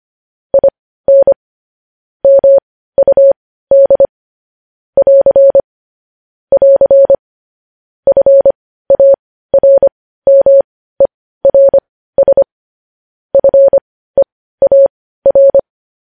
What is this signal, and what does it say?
Morse code, heard as a keyed tone, about 25 wpm, spaced as for 15 wpm, 550 Hz. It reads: IN MUD <AR> <AR> FARMERS FEAR
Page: news-headlines-in-morse-code-at-15-wpm